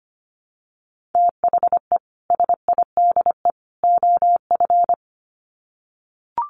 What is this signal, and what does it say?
Morse code, heard as a keyed tone, about 25 wpm, 700 Hz. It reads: THE SIDE OF
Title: Sets of 3 Words 25wpm